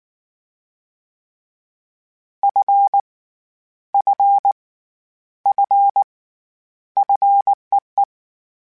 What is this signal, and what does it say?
Morse code, heard as a keyed tone, about 19 wpm, 800 Hz. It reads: F F F FEE
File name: Short Runs / Short Runs-mp3s / SinglePrefixes @18